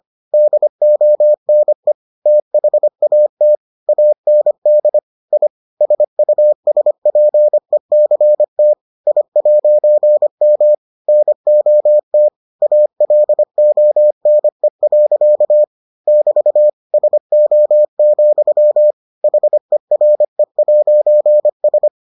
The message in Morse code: DONE THAT AND I SUSPECT I'M NOT ALONE. <BT> SO, HERE'S

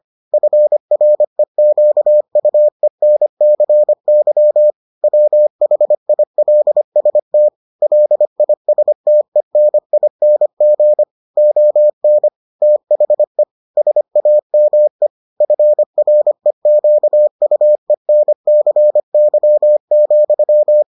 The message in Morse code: FREQUENCY WHILST LISTENING ON THE SAME FREQUENCY,

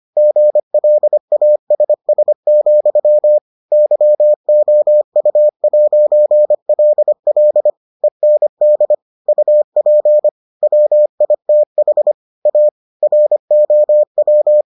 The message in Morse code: GLASS, YOU'LL END UP WITH A ROW

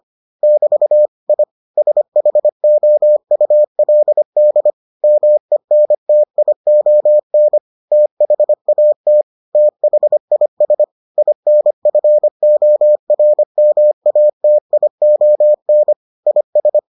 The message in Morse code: <BT> I SHOULD MENTION THAT THIS INFORMATION IS

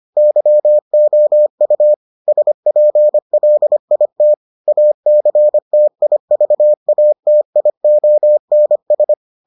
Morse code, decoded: YOU SPLIT ACTIVATIONS

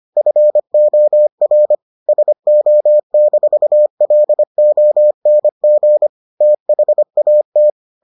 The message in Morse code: FOR SO-LONG THAT

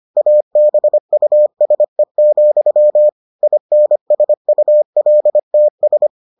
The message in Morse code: ABUSE, INSULTS

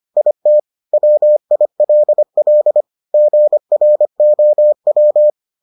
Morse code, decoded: IT WILL GROW

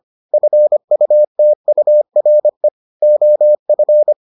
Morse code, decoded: FUTURE OF

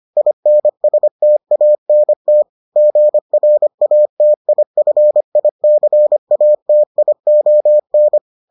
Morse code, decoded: INSTANT GRATIFICATION

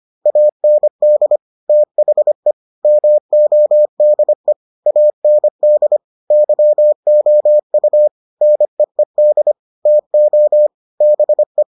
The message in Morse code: AND THE MODE AND YOU NEED TO BE

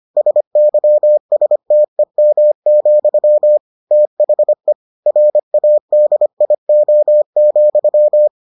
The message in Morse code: SYSTEM, THE RADIO,

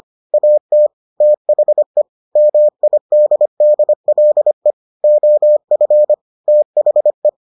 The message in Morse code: AT THE MIDDLE OF THE

